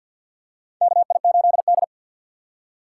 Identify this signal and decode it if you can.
Morse code, heard as a keyed tone, about 48 wpm, 700 Hz. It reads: KI8D
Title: USA Call Signs - 4 Characters - 6x Character Spacing - 50wpm